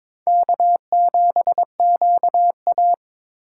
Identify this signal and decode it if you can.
Morse code, heard as a keyed tone, about 22 wpm, 700 Hz. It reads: K7QA
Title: USA Call Signs - 4 Characters - 3x Character Spacing - 22wpm